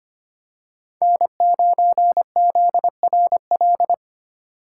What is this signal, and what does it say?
Morse code, heard as a keyed tone, about 25 wpm, 700 Hz. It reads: N9ZRL
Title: USA Call Signs - 5 Characters - 4x Character Spacing - 25wpm